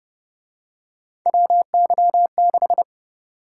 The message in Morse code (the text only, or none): WY6